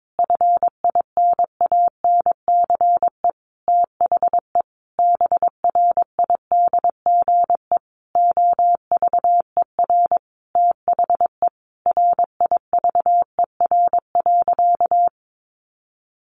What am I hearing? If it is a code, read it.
Morse code, heard as a keyed tone, about 22 wpm, 700 Hz. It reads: FINANCE THE BRIDGE OVER THE RIVER.
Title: Sentences from Top 1500 Words 22wpm